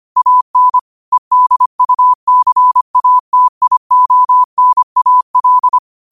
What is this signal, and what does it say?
Morse code, heard as a keyed tone, about 25 wpm, 1000 Hz. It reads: AN EDUCATIONAL
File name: qotdfiles / 25wpm